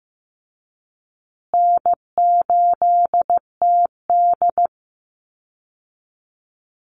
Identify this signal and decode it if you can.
Morse code, heard as a keyed tone, about 15 wpm, 700 Hz. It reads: N8TD
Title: Call Signs Repeated 3x - 15wpm